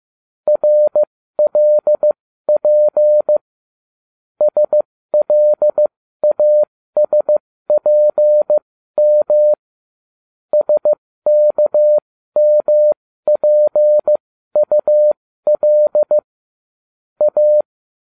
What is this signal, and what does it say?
Morse code, heard as a keyed tone, about 15 wpm, spaced as for 13 wpm, 600 Hz. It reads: RLP SLASPM SKMPUL A